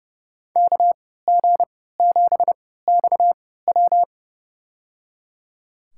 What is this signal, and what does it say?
Morse code, heard as a keyed tone, about 29 wpm, spaced as for 17 wpm, 700 Hz. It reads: KG7XW
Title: USA Call Signs - 5 Characters - 3x Character Spacing - 30wpm